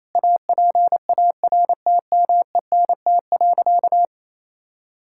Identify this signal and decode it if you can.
Morse code, heard as a keyed tone, about 28 wpm, 700 Hz. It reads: APARTMENT.
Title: Sentences from Top 1000 Words - Review Part 6 - 28wpm